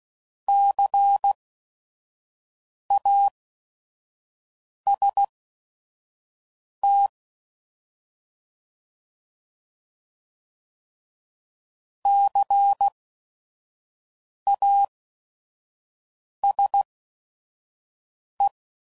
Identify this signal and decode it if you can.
Morse code, heard as a keyed tone, about 16 wpm, spaced as for 5 wpm, 800 Hz. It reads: CAST CASE
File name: Prefixes / By First Letter / Cs